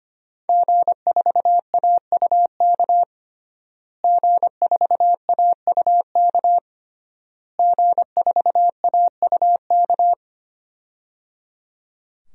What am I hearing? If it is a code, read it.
Morse code, heard as a keyed tone, about 25 wpm, 700 Hz. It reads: G4AUK G4AUK G4AUK
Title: DXCC-100 Call Signs - 5 Characters - Repeated 3x - 25wpm